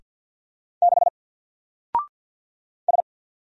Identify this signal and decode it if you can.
Morse code, heard as a keyed tone, about 50 wpm, 700 Hz. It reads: 6 S
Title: Single Letter-Number Alt 50wpm